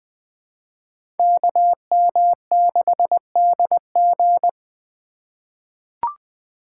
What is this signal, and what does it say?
Morse code, heard as a keyed tone, about 20 wpm, 700 Hz. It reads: KM6DG
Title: USA Call Signs - 4x Character Spacing - 20wpm